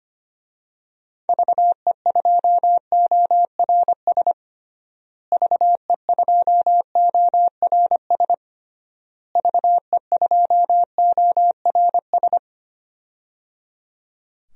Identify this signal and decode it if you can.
Morse code, heard as a keyed tone, about 25 wpm, 700 Hz. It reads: VE2ORS VE2ORS VE2ORS
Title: DXCC-100 Call Signs - 6 Characters - Repeated 3x - 25wpm